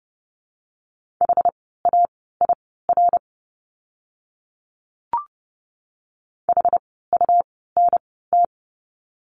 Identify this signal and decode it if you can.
Morse code, heard as a keyed tone, about 30 wpm, spaced as for 17 wpm, 700 Hz. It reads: HAIR HUNT
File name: Lesson 17f - Letter u - Words 30wpm